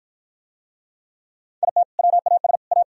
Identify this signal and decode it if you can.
Morse code, heard as a keyed tone, about 48 wpm, 700 Hz. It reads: IT WAS A